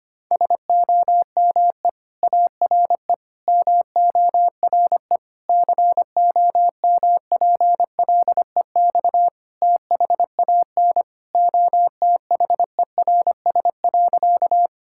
Morse code, heard as a keyed tone, about 25 wpm, 700 Hz. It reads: SOME ARE MORE COMPLEX THAN OTHERS.
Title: Sentences from Top 1000 Words - Review Part 6 - 25wpm